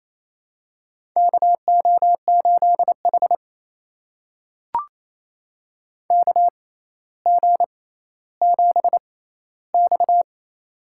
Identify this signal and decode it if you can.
Morse code, heard as a keyed tone, about 28 wpm, spaced as for 14 wpm, 700 Hz. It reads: KO8H KG7X
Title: USA Call Signs - 4 Characters - 6x Character Spacing - 28wpm